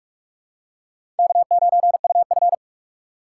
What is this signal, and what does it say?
Morse code, heard as a keyed tone, about 46 wpm, 700 Hz. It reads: K9UF
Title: USA Call Signs - 4 Characters - 3x Character Spacing - 45wpm